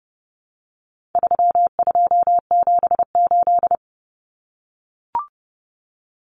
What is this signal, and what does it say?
Morse code, heard as a keyed tone, about 30 wpm, 700 Hz. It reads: 3278
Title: Common Cut Numbers - 4-Digit 30wpm